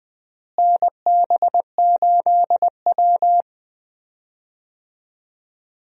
Morse code, heard as a keyed tone, about 20 wpm, 700 Hz. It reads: NB8W
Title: USA Call Signs - 4 Characters - Repeated 2x - 20wpm